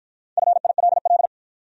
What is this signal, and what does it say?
Morse code, heard as a keyed tone, about 53 wpm, 700 Hz. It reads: FILL